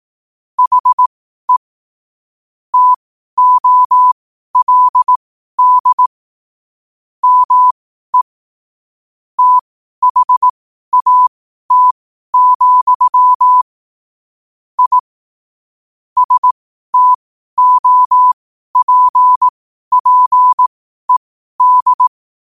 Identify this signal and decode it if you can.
Morse code, heard as a keyed tone, about 18 wpm, spaced as for 12 wpm, 1000 Hz. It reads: HE TOLD ME THAT, I STOPPED
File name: qotdfiles / 12wpm